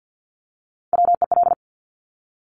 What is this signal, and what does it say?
Morse code, heard as a keyed tone, about 50 wpm, 700 Hz. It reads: WEL
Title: Top Three Letter Combinations in Words 50wpm